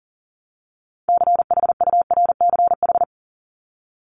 Morse code, heard as a keyed tone, about 40 wpm, 700 Hz. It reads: CHURCH
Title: Top 1751-2000 Words 40wpm